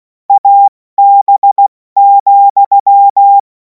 Morse code, heard as a keyed tone, about 16 wpm, 800 Hz. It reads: AB,
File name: Morsekurs / Witze / Witz 31. März 2026 70 F80